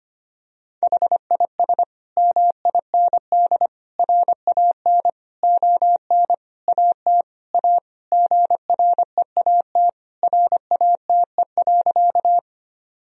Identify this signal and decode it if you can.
Morse code, heard as a keyed tone, about 25 wpm, 700 Hz. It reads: HIS MIND RAN ON AT A GREAT RATE.
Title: Sentences from Top 1000 Words - Review Part 4 - 25wpm